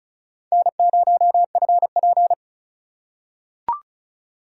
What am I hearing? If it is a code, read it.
Morse code, heard as a keyed tone, about 34 wpm, 700 Hz. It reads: N0FP
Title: USA Call Signs - 4 Characters - 3x Character Spacing - 35wpm